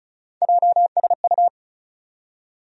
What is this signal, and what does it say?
Morse code, heard as a keyed tone, about 36 wpm, 700 Hz. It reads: JSU
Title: USA Call Sign Postfixes - 3 Letters 35wpm